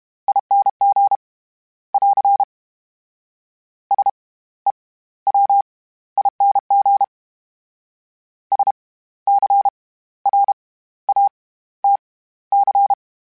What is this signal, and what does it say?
Morse code, heard as a keyed tone, about 32 wpm, spaced as for 16 wpm, 800 Hz. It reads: ING <AR> SEWING SCRATC
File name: Special INGs / Special INGs-mp3s / SlowToFaster ing @30